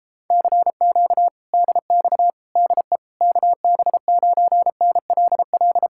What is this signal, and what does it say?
Morse code, heard as a keyed tone, about 32 wpm, 700 Hz. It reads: CQ DX DE KB9NLL